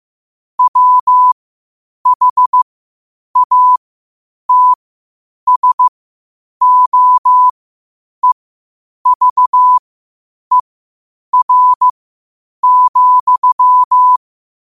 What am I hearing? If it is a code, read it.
Morse code, heard as a keyed tone, about 15 wpm, spaced as for 8 wpm, 1000 Hz. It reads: WHATSOEVER,
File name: qotdfiles / 8wpm